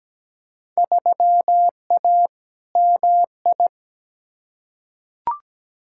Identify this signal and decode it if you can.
Morse code, heard as a keyed tone, about 17 wpm, 700 Hz. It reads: 3A MI
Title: Field Day Exchanges - 17wpm